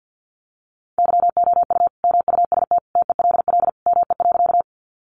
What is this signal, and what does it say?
Morse code, heard as a keyed tone, about 50 wpm, 700 Hz. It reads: YOU MUST TELL ME.
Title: Sentences from Top 200 Words 50wpm